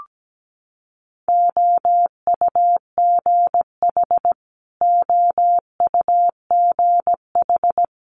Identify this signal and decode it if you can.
Morse code, heard as a keyed tone, about 17 wpm, 700 Hz. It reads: OUGH OUGH
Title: Multi-letter Phonemes Repeated 3x 17wpm